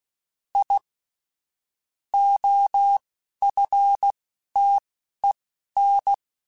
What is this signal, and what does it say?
Morse code, heard as a keyed tone, about 16 wpm, spaced as for 11 wpm, 800 Hz. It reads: I OFTEN
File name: QSOs / QSOs-mp3s / kc7haa@10